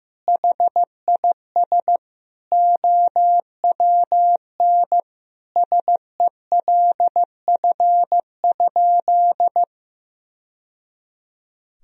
Morse code, heard as a keyed tone, about 15 wpm, 700 Hz. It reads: HIS OWN SELF?
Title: Sentences from Top 1000 Words - Review Part 1 - 15wpm